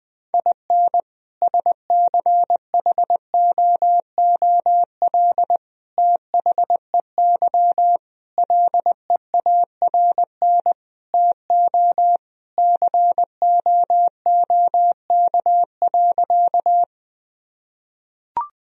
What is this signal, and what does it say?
Morse code, heard as a keyed tone, about 20 wpm, 700 Hz. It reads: IN SCHOOL THEY LEARN TO COOK.
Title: Sentences from Top 1000 Words - Review Part 4 - 20wpm